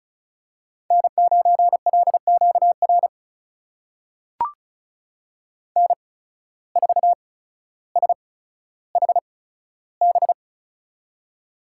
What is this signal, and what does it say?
Morse code, heard as a keyed tone, about 36 wpm, spaced as for 15 wpm, 700 Hz. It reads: N9LQR N4SHB
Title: USA Call Signs - 8x Character Spacing - 35wpm